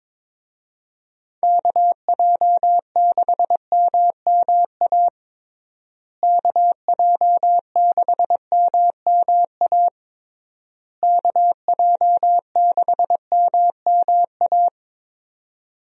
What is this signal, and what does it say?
Morse code, heard as a keyed tone, about 22 wpm, 700 Hz. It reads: KJ6MMA KJ6MMA KJ6MMA
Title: USA Call Signs - 6 Characters - Repeated 3x - 22wpm